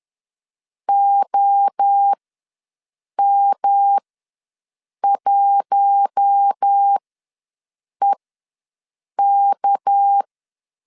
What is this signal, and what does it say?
Morse code, heard as a keyed tone, about 10 wpm, spaced as for 6 wpm, 800 Hz. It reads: OM1EK
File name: LEZIONI / 7 GF9Z / 7 GF9 ws 10wpm 4034